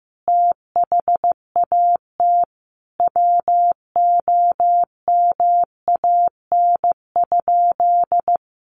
Morse code, heard as a keyed tone, about 15 wpm, 700 Hz. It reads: THAT WOMAN?
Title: Sentences from Top 1100 Words 15wpm